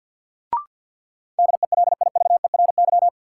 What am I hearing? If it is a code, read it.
Morse code, heard as a keyed tone, about 53 wpm, 700 Hz. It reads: DELIVERY
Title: Sentences from Top 1500 Words - New Word Drill Alt - 50wpm